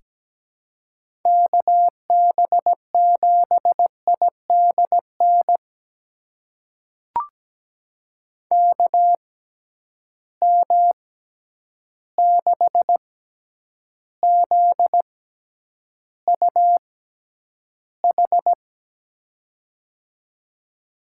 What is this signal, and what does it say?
Morse code, heard as a keyed tone, about 17 wpm, spaced as for 9 wpm, 700 Hz. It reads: KB7IDN KM6ZUH
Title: USA Call Signs - 6 Characters - 6x Character Spacing - 17wpm